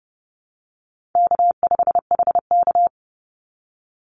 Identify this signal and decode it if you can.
Morse code, heard as a keyed tone, about 30 wpm, 700 Hz. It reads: K5HK
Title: Easy Call Signs Repeated 2x - 30wpm